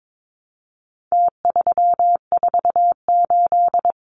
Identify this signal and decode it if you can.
Morse code, heard as a keyed tone, about 22 wpm, 700 Hz. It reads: T348
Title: Numbers - Contest Style - 4-Digit 22wpm